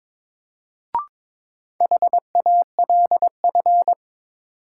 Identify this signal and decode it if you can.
Morse code, heard as a keyed tone, about 21 wpm, 700 Hz. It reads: HALF
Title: Sets of 3 Words - New Word Drill - 22wpm